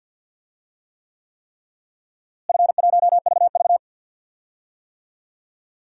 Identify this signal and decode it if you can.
Morse code, heard as a keyed tone, about 47 wpm, 700 Hz. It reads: F1VV
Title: DXCC-100 Call Signs - 4 Characters - Repeated 3x - 50wpm